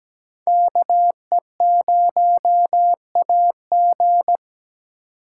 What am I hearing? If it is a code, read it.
Morse code, heard as a keyed tone, about 22 wpm, 700 Hz. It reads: KE0AG